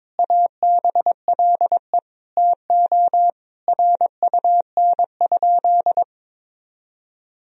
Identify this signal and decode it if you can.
Morse code, heard as a keyed tone, about 21 wpm, 700 Hz. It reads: ABLE TO RUN?
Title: Sentences from Top 1000 Words - Review Part 2 - 22wpm